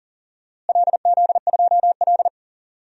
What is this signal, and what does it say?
Morse code, heard as a keyed tone, about 40 wpm, 700 Hz. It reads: LZ2L
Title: DXCC-100 Call Signs - 4 Characters - Repeated 3x - 40wpm